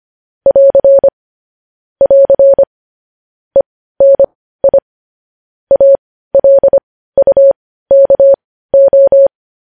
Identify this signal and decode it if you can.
Morse code, heard as a keyed tone, about 25 wpm, spaced as for 15 wpm, 550 Hz. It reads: <AR> <AR> ENI ALUKO
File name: podcast / News headlines-1x-25x15